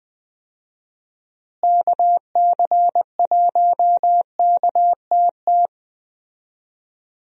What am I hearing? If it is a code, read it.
Morse code, heard as a keyed tone, about 20 wpm, 700 Hz. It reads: KC1KTT
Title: USA Call Signs - 6 Characters - Repeated 3x - 20wpm